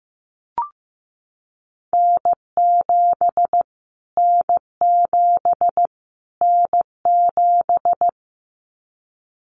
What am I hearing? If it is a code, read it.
Morse code, heard as a keyed tone, about 15 wpm, 700 Hz. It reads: N7 N7 N7
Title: Callsign Prefixes for Top 25 DXCC - Repeated 3x - 15wpm